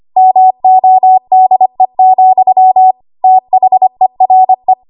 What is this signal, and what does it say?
Morse code, heard as a keyed tone, about 25 wpm, 750 Hz. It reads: MODE, THERE
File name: Archive / 25 WPM / 25WPM